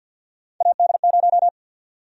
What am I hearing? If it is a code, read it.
Morse code, heard as a keyed tone, about 53 wpm, 700 Hz. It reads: AD0
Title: USA Call Sign Prefixes - Repeated 2x - 50wpm